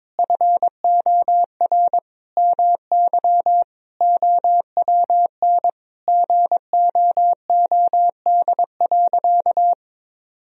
Morse code, heard as a keyed tone, about 21 wpm, 700 Hz. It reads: FOR MY OWN GOOD.
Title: Sentences from Top 1000 Words - Review Part 4 - 22wpm